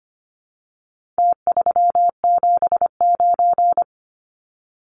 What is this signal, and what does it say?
Morse code, heard as a keyed tone, about 25 wpm, 700 Hz. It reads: T379
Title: Numbers - Contest Style - 4-Digit 25wpm